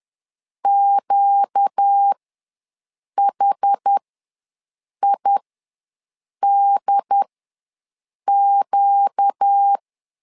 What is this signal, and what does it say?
Morse code, heard as a keyed tone, about 10 wpm, spaced as for 6 wpm, 800 Hz. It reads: QHIDQ
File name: LEZIONI / 4 Q ws 10wpm 2306